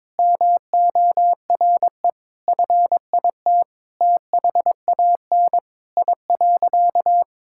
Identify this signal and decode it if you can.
Morse code, heard as a keyed tone, about 22 wpm, 700 Hz. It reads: MORE FIT THAN I.
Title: Sentences from Top 1000 Words - Review Part 5 - 22wpm